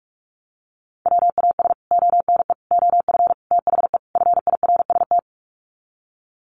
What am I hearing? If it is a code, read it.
Morse code, heard as a keyed tone, about 44 wpm, 700 Hz. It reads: WAS ONE OF THE FIRST
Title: Sets of 5 Words 45wpm